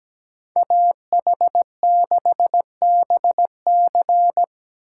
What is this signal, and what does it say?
Morse code, heard as a keyed tone, about 17 wpm, 700 Hz. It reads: AH6BC
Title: USA Call Signs - 8x Character Spacing - 17wpm